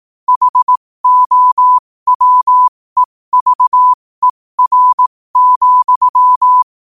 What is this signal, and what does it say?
Morse code, heard as a keyed tone, about 18 wpm, spaced as for 15 wpm, 1000 Hz. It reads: HOWEVER,
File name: qotdfiles / 15wpm